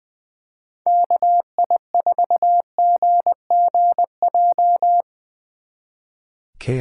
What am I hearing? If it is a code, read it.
Morse code, heard as a keyed tone, about 23 wpm, 700 Hz. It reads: KI4GGJ
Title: Speed Racing - USA Call Signs 20wpm